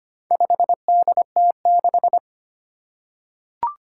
Morse code, heard as a keyed tone, about 25 wpm, 700 Hz. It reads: 5DT6
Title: Four Challenging Letter-Number Combinations 25wpm